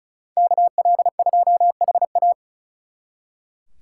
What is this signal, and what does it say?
Morse code, heard as a keyed tone, about 43 wpm, 700 Hz. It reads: KL2HA
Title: Speed Racing - USA 2x2 Call Signs 35wpm